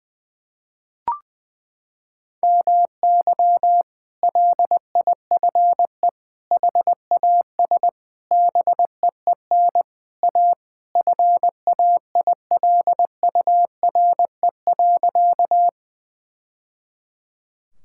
Morse code, heard as a keyed tone, about 21 wpm, 700 Hz. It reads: MY LIFE HAS BEEN A FAILURE.
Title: Sentences from Top 1000 Words - Review Part 6 - 20wpm